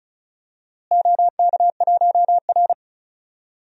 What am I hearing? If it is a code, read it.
Morse code, heard as a keyed tone, about 34 wpm, 700 Hz. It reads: OK1R
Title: DXCC-100 Call Signs - 4 Characters - Repeated 2x - 35wpm